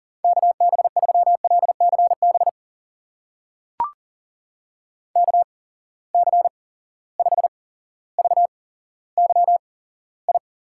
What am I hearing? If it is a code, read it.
Morse code, heard as a keyed tone, about 38 wpm, spaced as for 17 wpm, 700 Hz. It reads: KB3LCB KC5VYI
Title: USA Call Signs - 6 Characters - 8x Character Spacing - 40wpm